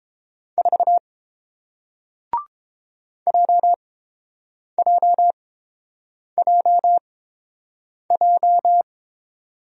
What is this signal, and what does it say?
Morse code, heard as a keyed tone, about 30 wpm, spaced as for 7 wpm, 700 Hz. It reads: 4 JJJJ